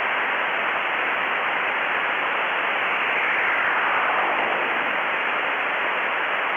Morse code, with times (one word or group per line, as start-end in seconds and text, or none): none